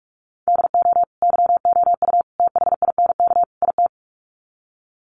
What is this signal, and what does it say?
Morse code, heard as a keyed tone, about 44 wpm, 700 Hz. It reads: DO YOU THINK IT
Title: Sets of 4 Words 45wpm